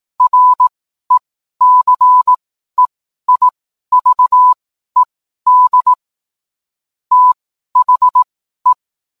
Morse code, heard as a keyed tone, about 18 wpm, spaced as for 12 wpm, 1000 Hz. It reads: RECEIVED THE